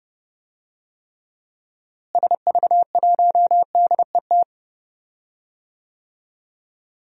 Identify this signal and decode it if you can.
Morse code, heard as a keyed tone, about 32 wpm, 700 Hz. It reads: SV1DET
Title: DXCC-100 Call Signs - 6 Characters - Repeated 3x - 30wpm